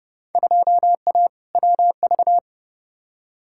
Morse code, heard as a keyed tone, about 30 wpm, 700 Hz. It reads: 2A WV
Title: Field Day Exchanges Encore - 30wpm